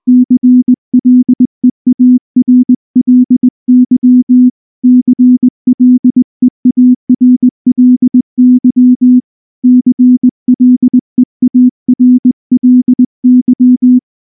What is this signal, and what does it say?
Morse code, heard as a keyed tone, about 20 wpm, 250 Hz. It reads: CLEARLY CLEARLY CLEARLY
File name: morse / morse-training-20WPM-9